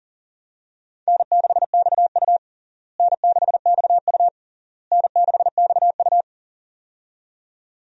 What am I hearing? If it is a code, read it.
Morse code, heard as a keyed tone, about 40 wpm, 700 Hz. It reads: N6XU N6XU N6XU
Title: USA Call Signs - 4 Characters - Repeated 3x - 40wpm